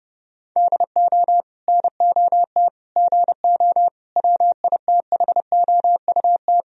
Morse code, heard as a keyed tone, about 32 wpm, 700 Hz. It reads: DO NOT GO WITHOUT
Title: Sentences from Top 1000 Words - Review Part 3 - 30wpm